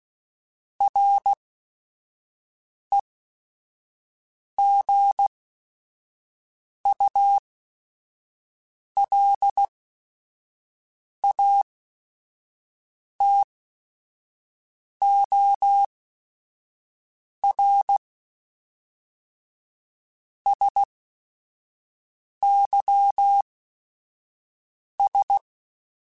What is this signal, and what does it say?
Morse code, heard as a keyed tone, about 16 wpm, spaced as for 5 wpm, 800 Hz. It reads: REGULATOR SYS
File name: Technical Subjects / Technical Subjects-mp3s / Sw Pwr Sup @5